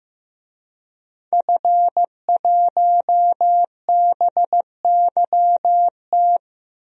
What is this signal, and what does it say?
Morse code, heard as a keyed tone, about 15 wpm, 700 Hz. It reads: F1BYT
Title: DXCC-100 Call Signs - 5 Characters - Repeated 3x - 15wpm